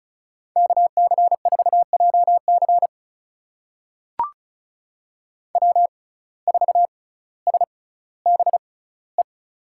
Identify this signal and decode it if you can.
Morse code, heard as a keyed tone, about 36 wpm, spaced as for 18 wpm, 700 Hz. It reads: KC4JC W4SBE
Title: USA Call Signs - 5 Characters - 6x Character Spacing - 35wpm